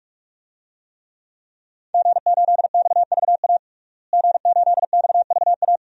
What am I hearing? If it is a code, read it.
Morse code, heard as a keyed tone, about 46 wpm, 700 Hz. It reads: G8XVA G8XVA
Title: DXCC-100 Call Signs - 5 Characters - Repeated 2x - 45wpm